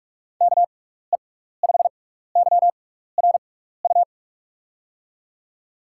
Morse code, heard as a keyed tone, about 44 wpm, spaced as for 15 wpm, 700 Hz. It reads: KE5YRU
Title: USA Call Signs - 6 Characters - 6x Character Spacing - 45wpm